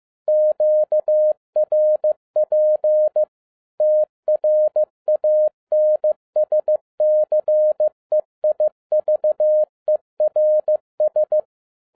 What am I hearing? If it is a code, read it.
Morse code, heard as a keyed tone, about 15 wpm, 610 Hz. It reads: QRP TRANSCEIVERS
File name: 2011-07-mp3 / n22 QRP OTAKU 15wpm